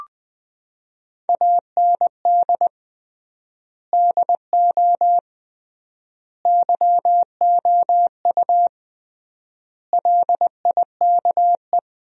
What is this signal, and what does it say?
Morse code, heard as a keyed tone, about 21 wpm, 700 Hz. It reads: AND DO YOU LIKE
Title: 3x Word-Spacing - Sentences from Top 100 Words - 20wpm